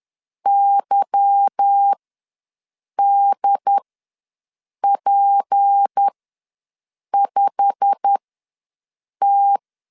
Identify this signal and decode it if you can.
Morse code, heard as a keyed tone, about 10 wpm, spaced as for 6 wpm, 800 Hz. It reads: YDP5T
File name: LEZIONI / 8 YJ45 / 8 YJ45 ws 10wpm 3843